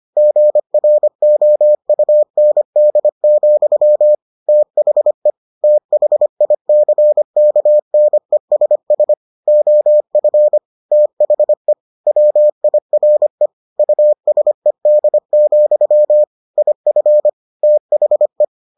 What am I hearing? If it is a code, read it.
Morse code, heard as a keyed tone, about 25 wpm, 600 Hz. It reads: GROUND, THE THICKNESS OF THE WIRE USED, IF THE